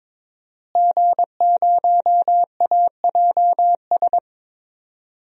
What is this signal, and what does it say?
Morse code, heard as a keyed tone, about 22 wpm, 700 Hz. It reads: G0AJS
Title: DXCC-100 Call Signs - 5 Characters - Repeated 3x - 22wpm